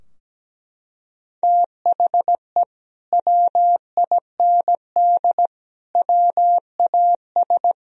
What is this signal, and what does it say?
Morse code, heard as a keyed tone, about 17 wpm, 700 Hz. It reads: THE WIND WAS
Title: Sentences from Top 1000 Words - Review Part 2 - 17wpm